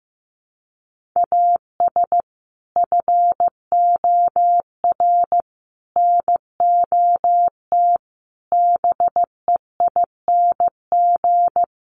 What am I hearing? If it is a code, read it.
Morse code, heard as a keyed tone, about 15 wpm, 700 Hz. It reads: AS FOR NOT BEING
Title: Sentences from Top 1100 Words 15wpm